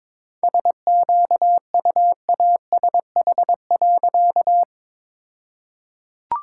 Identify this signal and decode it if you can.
Morse code, heard as a keyed tone, about 22 wpm, 700 Hz. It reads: SQUASH.
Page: morse-code-ninja-practice-22wpm